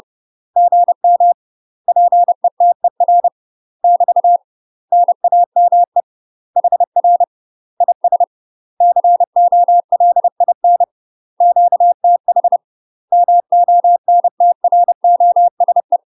Morse code, heard as a keyed tone, about 30 wpm, 700 Hz. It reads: GM PETER <BT> NAME HR IS COLIN QTH MONTROSE
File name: UK MASTER QSOs TRAINING / 30WPM / UK MASTER QSOs 0050